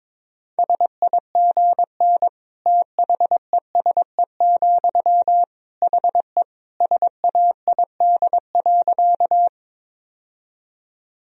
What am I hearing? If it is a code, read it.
Morse code, heard as a keyed tone, about 22 wpm, 700 Hz. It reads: SIGN THESE, HE SAID.
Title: Sentences from Top 1000 Words - Review Part 1 - 22wpm